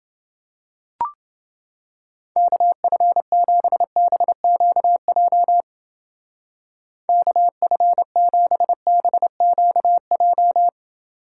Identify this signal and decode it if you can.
Morse code, heard as a keyed tone, about 28 wpm, 700 Hz. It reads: KF7BQJ KF7BQJ